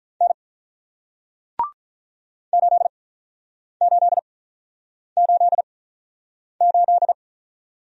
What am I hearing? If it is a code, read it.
Morse code, heard as a keyed tone, about 48 wpm, spaced as for 9 wpm, 700 Hz. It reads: N 8888